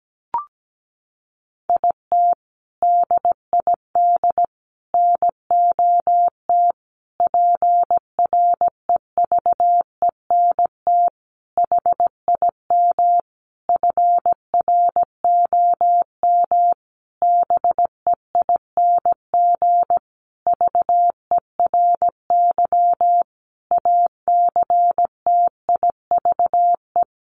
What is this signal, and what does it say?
Morse code, heard as a keyed tone, about 17 wpm, 700 Hz. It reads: IT DID NOT PREVENT HIM FROM BEING VERY ACTIVE
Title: Sentences from Top 1500 Words 17wpm